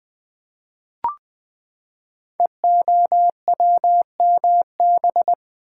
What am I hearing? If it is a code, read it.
Morse code, heard as a keyed tone, about 20 wpm, 700 Hz. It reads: EOWMB
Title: Code Groups - 5 Letter:Numbers - 20wpm